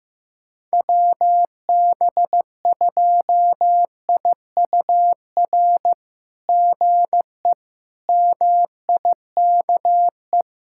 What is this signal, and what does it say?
Morse code, heard as a keyed tone, about 15 wpm, 700 Hz. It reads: WB2IUR GE MIKE
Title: SOTA QSO Practice 15wpm